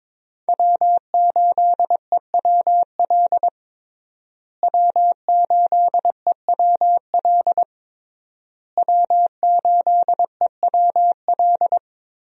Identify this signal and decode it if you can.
Morse code, heard as a keyed tone, about 22 wpm, 700 Hz. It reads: W8EWL W8EWL W8EWL
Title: USA Call Signs - 5 Characters - Repeated 3x - 22wpm